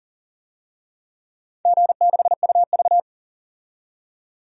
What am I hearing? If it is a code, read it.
Morse code, heard as a keyed tone, about 40 wpm, 690 Hz. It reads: G6UV
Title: DXCC-100 Call Signs - 4 Characters - 40wpm